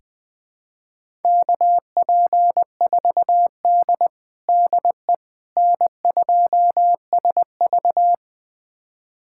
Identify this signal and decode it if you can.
Morse code, heard as a keyed tone, about 21 wpm, 700 Hz. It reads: KP4D DE N2SV
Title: USA Callsign DE Callsign - 4 Characters - 20wpm